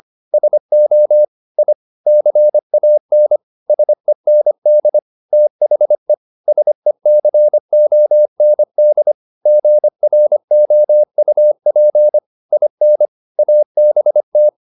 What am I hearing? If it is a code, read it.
Morse code, heard as a keyed tone, about 25 wpm, 600 Hz. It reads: SO I CAN SEND THE SECOND GROUP IN ABT
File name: SS220 25